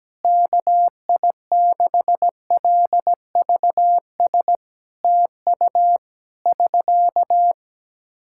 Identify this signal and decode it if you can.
Morse code, heard as a keyed tone, about 17 wpm, 700 Hz. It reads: KI6LVS TU <SK>
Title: Practice QSO - Exchange 4 - Ending - 17wpm